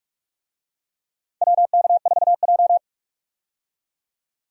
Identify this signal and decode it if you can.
Morse code, heard as a keyed tone, about 46 wpm, 700 Hz. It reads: WK4J
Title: USA Call Signs - 4 Characters - 45wpm